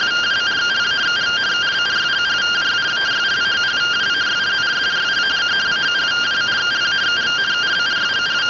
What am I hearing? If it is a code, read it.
Two-tone teleprinter signal, about 170 Hz shift.